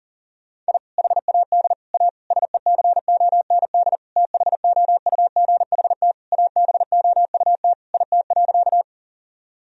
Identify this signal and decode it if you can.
Morse code, heard as a keyed tone, about 40 wpm, 700 Hz. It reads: I HAD A SECOND THOUGHT ABOUT IT.
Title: Sentences from Top 1000 Words - Review Part 1 - 40wpm